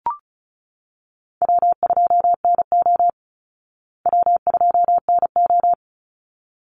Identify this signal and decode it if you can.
Morse code, heard as a keyed tone, about 34 wpm, 700 Hz. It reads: W2NO W2NO
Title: Call Signs Repeated 3x - 35wpm